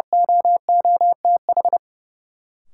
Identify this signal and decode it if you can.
Morse code, heard as a keyed tone, about 30 wpm, 700 Hz. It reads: OOTH
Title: Sentences from Top 1400 Words - New Word Drill - 30wpm